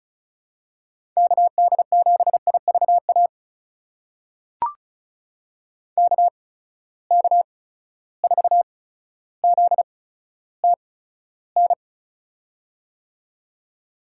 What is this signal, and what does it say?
Morse code, heard as a keyed tone, about 36 wpm, spaced as for 15 wpm, 700 Hz. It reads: KD7IVA KK4ZTN
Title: USA Call Signs - 6 Characters - 8x Character Spacing - 35wpm